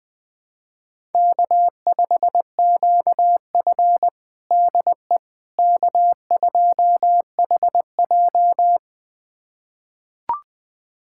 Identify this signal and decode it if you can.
Morse code, heard as a keyed tone, about 20 wpm, 700 Hz. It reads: K5QF DE K2HJ
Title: USA Callsign DE Callsign - 4 Characters - 20wpm